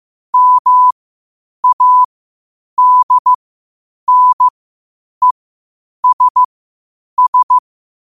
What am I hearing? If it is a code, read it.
Morse code, heard as a keyed tone, about 15 wpm, spaced as for 8 wpm, 1000 Hz. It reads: MADNESS
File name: qotdfiles / 8wpm